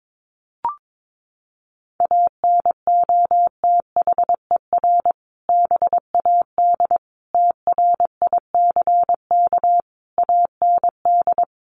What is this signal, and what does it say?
Morse code, heard as a keyed tone, about 22 wpm, 700 Hz. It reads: ANOTHER BAD TRICK AND
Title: Sentences from Top 1500 Words 22wpm